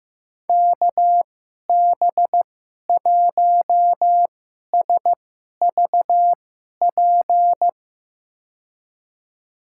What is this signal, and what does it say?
Morse code, heard as a keyed tone, about 15 wpm, spaced as for 11 wpm, 700 Hz. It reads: KB1SVP
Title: USA Call Signs - 6 Characters - 2x Character Spacing - 15wpm